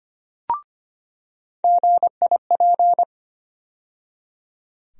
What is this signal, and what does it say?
Morse code, heard as a keyed tone, about 25 wpm, 700 Hz. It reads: GIP
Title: USA Call Sign Postfixes - 3 Letters 25wpm